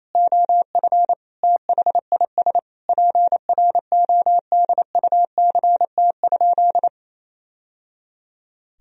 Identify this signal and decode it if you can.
Morse code, heard as a keyed tone, about 28 wpm, 700 Hz. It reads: OF THIS PRODUCT?
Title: Sentences from Top 1000 Words - Review Part 3 - 28wpm